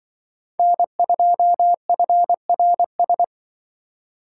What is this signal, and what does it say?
Morse code, heard as a keyed tone, about 25 wpm, 700 Hz. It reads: N2FRS